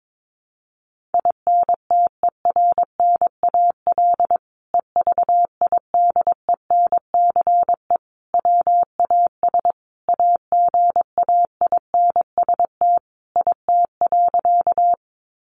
Morse code, heard as a keyed tone, about 22 wpm, 700 Hz. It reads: INTERNAL EVIDENCE WAS AGAINST IT.
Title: Sentences from Top 1400 Words 22wpm